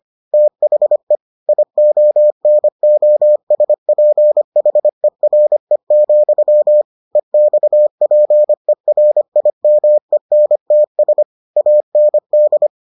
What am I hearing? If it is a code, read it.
Morse code, heard as a keyed tone, about 25 wpm, 600 Hz. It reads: THE IONOSPHERE, EXPERIMENTS AND